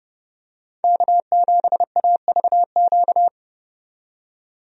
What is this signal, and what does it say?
Morse code, heard as a keyed tone, about 32 wpm, 700 Hz. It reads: K7AVQ
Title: USA Call Signs - 5 Characters - 30wpm